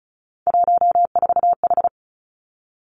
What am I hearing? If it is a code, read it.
Morse code, heard as a keyed tone, about 34 wpm, 700 Hz. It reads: 14H
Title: Three Challenging Letter-Number Combinations 35wpm